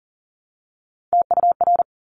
Morse code, heard as a keyed tone, about 39 wpm, 700 Hz. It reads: TUR
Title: Top Three Letter Combinations in Words 40wpm